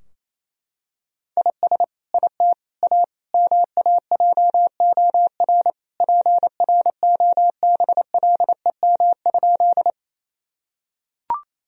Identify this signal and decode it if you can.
Morse code, heard as a keyed tone, about 28 wpm, 700 Hz. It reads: IS IT A MAJOR PROBLEM?
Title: Sentences from Top 1000 Words - Review Part 4 - 28wpm